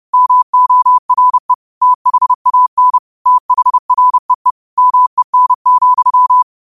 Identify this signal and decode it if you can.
Morse code, heard as a keyed tone, about 31 wpm, 1000 Hz. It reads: MORE THAN THREE MEN,
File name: qotdfiles / 30wpm